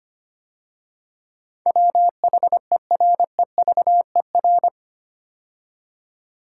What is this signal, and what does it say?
Morse code, heard as a keyed tone, about 25 wpm, 690 Hz. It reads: WHEREVER
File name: 8 Letter Words - First 3 Letters Given - 25wpm